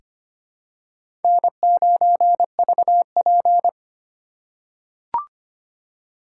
Morse code, heard as a keyed tone, about 25 wpm, 700 Hz. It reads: N9VP
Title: USA Call Signs - 4 Characters - 4x Character Spacing - 25wpm